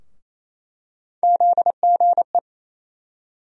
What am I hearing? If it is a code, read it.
Morse code, heard as a keyed tone, about 28 wpm, 700 Hz. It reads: ZGE
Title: USA Call Sign Postfixes - 3 Letters 28wpm